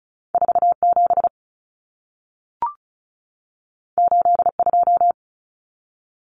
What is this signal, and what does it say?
Morse code, heard as a keyed tone, about 36 wpm, 700 Hz. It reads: 47 82
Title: Common Cut Numbers - 2-Digit 35wpm